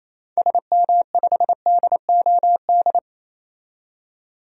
Morse code, heard as a keyed tone, about 28 wpm, 700 Hz. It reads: SM5DOD
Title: DXCC-100 Call Signs - 6 Characters - 28wpm